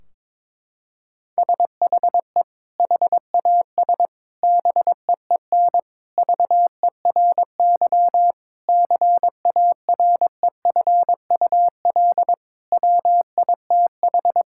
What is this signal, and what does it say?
Morse code, heard as a keyed tone, about 22 wpm, 700 Hz. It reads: SHE HAS BEEN VERY CAREFUL WITH
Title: Sentences from Top 1600 Words 22wpm